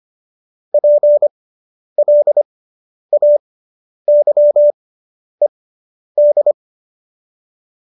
Morse code, heard as a keyed tone, about 25 wpm, spaced as for 10 wpm, 590 Hz. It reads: PLAYED